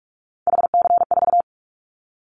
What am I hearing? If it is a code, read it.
Morse code, heard as a keyed tone, about 44 wpm, 700 Hz. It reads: HC4
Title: Callsign Prefixes for Top 100 DXCC - 45wpm